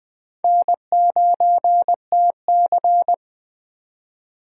Morse code, heard as a keyed tone, about 20 wpm, 700 Hz. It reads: N9TC
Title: Speed Racing - USA 1x2 Call Signs 17wpm